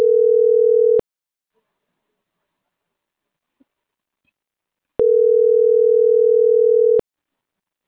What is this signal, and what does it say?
Ringback tone (North America), 440 + 480 Hz.